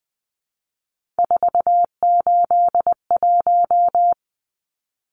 Morse code, heard as a keyed tone, about 20 wpm, 700 Hz. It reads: 481
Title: Numbers - 3-Digit 20wpm